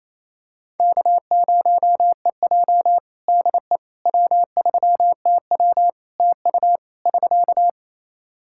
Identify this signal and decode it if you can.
Morse code, heard as a keyed tone, about 28 wpm, 700 Hz. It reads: K0EJ DE W3TW TU <SK>
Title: Practice QSO - Exchange 4 - Ending - 28wpm